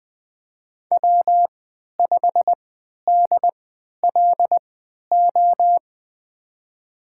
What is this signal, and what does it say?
Morse code, heard as a keyed tone, about 20 wpm, spaced as for 11 wpm, 700 Hz. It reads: W5DLO
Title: USA Call Signs - 3x Character Spacing - 20wpm